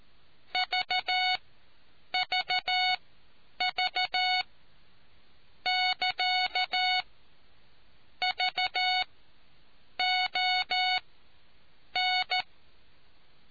Morse code, heard as a keyed tone, about 14 wpm, spaced as for 7 wpm, 740 Hz. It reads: VVV <KA> VON